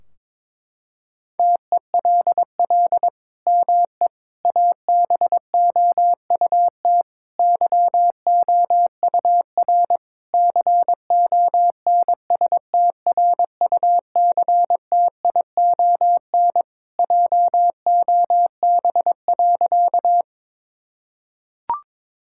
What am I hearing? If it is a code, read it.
Morse code, heard as a keyed tone, about 22 wpm, 700 Hz. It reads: TELL ME ABOUT YOUR CONSTRUCTION JOB.
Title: Sentences from Top 1600 Words 22wpm